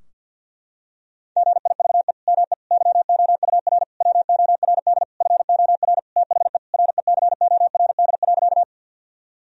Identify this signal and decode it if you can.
Morse code, heard as a keyed tone, about 50 wpm, 700 Hz. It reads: GIVE ME YOUR WORD FOR THE RECORD.
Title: Sentences from Top 1000 Words - Review Part 3 - 50wpm